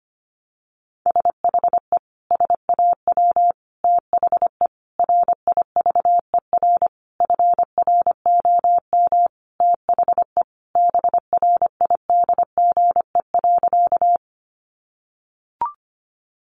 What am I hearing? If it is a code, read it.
Morse code, heard as a keyed tone, about 25 wpm, 700 Hz. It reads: SHE SAW THE RIVER FROM THE BRIDGE.
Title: Sentences from Top 1200 Words 25wpm